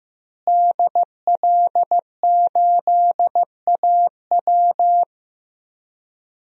Morse code, heard as a keyed tone, about 15 wpm, 700 Hz. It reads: DL8AW
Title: DXCC-100 Call Signs - 5 Characters - Repeated 2x - 15wpm